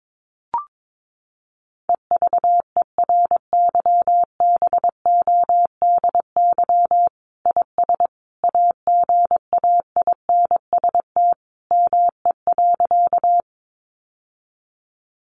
Morse code, heard as a keyed tone, about 22 wpm, 700 Hz. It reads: EVERYBODY IS AGAINST ME.
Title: Sentences from Top 1400 Words 22wpm